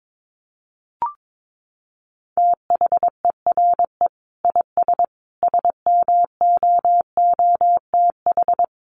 Morse code, heard as a keyed tone, about 22 wpm, 700 Hz. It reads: THERE IS SMOOTH
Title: Sentences from Top 1400 Words 22wpm